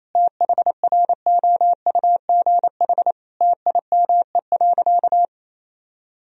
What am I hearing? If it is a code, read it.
Morse code, heard as a keyed tone, about 28 wpm, 700 Hz. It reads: THROUGH TIME.
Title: Sentences from Top 1000 Words - Review Part 2 - 28wpm